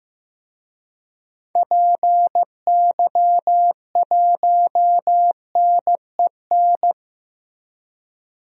Morse code, heard as a keyed tone, about 15 wpm, 700 Hz. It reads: PY1NEN
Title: DXCC-100 Call Signs - 6 Characters - Repeated 3x - 15wpm